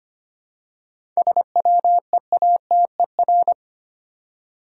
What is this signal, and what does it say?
Morse code, heard as a keyed tone, about 25 wpm, 700 Hz. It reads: SWEATER
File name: practice / 25wpm / Difficult Words with Hints - 25wpm